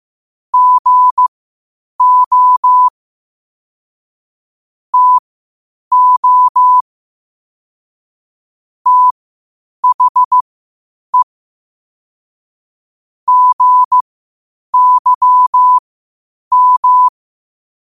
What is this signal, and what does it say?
Morse code, heard as a keyed tone, about 15 wpm, spaced as for 8 wpm, 1000 Hz. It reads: GO TO THE GYM
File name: qotdfiles / 8wpm